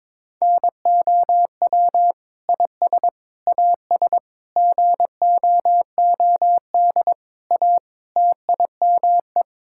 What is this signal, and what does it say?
Morse code, heard as a keyed tone, about 22 wpm, 700 Hz. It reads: NOW IS AS GOOD A TIME
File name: practice / 22wpm / Sentences with 8-Word Phrases - 22wpm